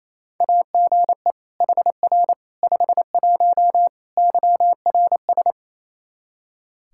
Morse code, heard as a keyed tone, about 27 wpm, 700 Hz. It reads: AGE HR 51 YRS
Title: Practice QSO - Exchange 3 - More Info - 28wpm